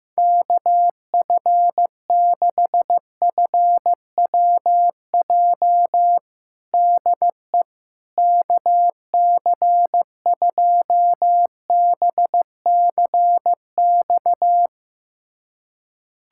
Morse code, heard as a keyed tone, about 15 wpm, 700 Hz. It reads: KF6FWJ DE KC2BCX
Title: USA Callsign DE Callsign - 6 Characters - 15wpm